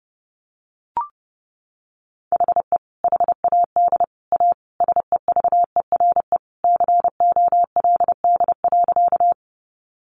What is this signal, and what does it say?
Morse code, heard as a keyed tone, about 30 wpm, 700 Hz. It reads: HE HAD A SEVERE COLD.
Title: Sentences from Top 1400 Words 30wpm